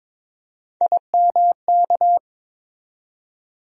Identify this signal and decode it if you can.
Morse code, heard as a keyed tone, about 22 wpm, 700 Hz. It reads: IMK
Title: USA Call Sign Postfixes - 3 Letters 22wpm